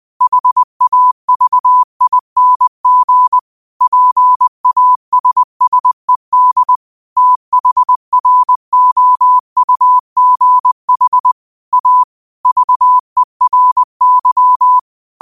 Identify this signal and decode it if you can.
Morse code, heard as a keyed tone, about 20 wpm, 1000 Hz. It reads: HAVING PASSED THROUGH A VERY
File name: qotdfiles / 20wpm